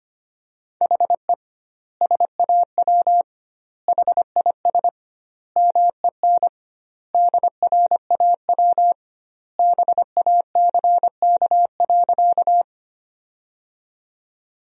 Morse code, heard as a keyed tone, about 25 wpm, 700 Hz. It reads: HE SAW HIS MEN DRAW BACK.
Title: 2x Word-Spacing - Sentences from Top 300 Words - 25wpm